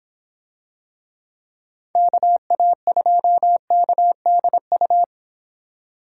Morse code, heard as a keyed tone, about 25 wpm, 700 Hz. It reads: KA2KDU